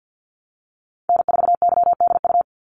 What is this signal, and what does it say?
Morse code, heard as a keyed tone, about 50 wpm, 700 Hz. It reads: N4YDU
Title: Call Signs Repeated 3x - 50wpm